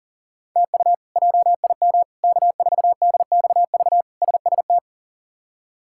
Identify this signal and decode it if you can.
Morse code, heard as a keyed tone, about 37 wpm, 700 Hz. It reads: TU JIM K4DXV SST
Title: K1USN SST QSO Practice - 40wpm